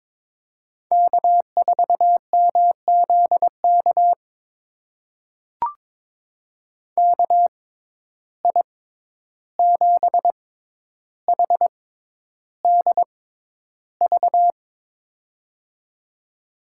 Morse code, heard as a keyed tone, about 21 wpm, spaced as for 11 wpm, 700 Hz. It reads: K4MZK KI7HDV
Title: USA Call Signs - 6x Character Spacing - 22wpm